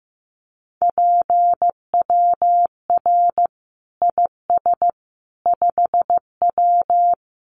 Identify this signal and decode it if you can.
Morse code, heard as a keyed tone, about 15 wpm, 700 Hz. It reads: PWR IS 5W
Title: Common QSO Phrases 15wpm